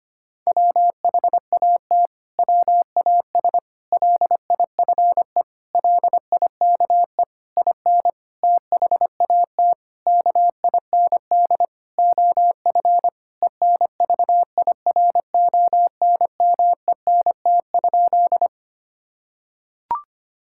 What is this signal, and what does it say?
Morse code, heard as a keyed tone, about 25 wpm, 700 Hz. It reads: WHAT WAS LIFE LIKE IN THAT KIND OF ENVIRONMENT?
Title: Sentences from Top 1000 Words - Review Part 5 - 25wpm